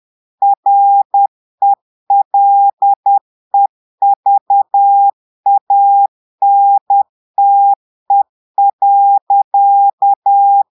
Morse code, heard as a keyed tone, about 10 wpm, 800 Hz. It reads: RELEVANTE.
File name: portugais texte en clair 2 vitesse 10 mots minutes